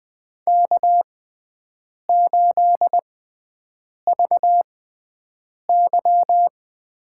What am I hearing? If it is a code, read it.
Morse code, heard as a keyed tone, about 20 wpm, spaced as for 7 wpm, 700 Hz. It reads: K8VY
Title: USA Call Signs - 4 Characters - 6x Character Spacing - 20wpm